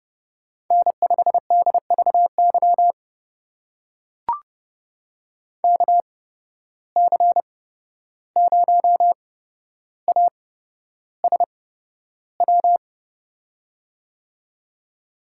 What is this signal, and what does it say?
Morse code, heard as a keyed tone, about 29 wpm, spaced as for 12 wpm, 700 Hz. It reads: N5DVY KC0ASW
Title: USA Call Signs - 8x Character Spacing - 30wpm